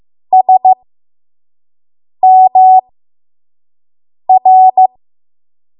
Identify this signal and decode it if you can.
Morse code, heard as a keyed tone, about 15 wpm, spaced as for 5 wpm, 750 Hz. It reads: SMR
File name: Morse / 05WPM